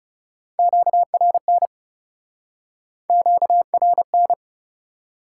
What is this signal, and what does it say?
Morse code, heard as a keyed tone, about 40 wpm, 700 Hz. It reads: QRN QRN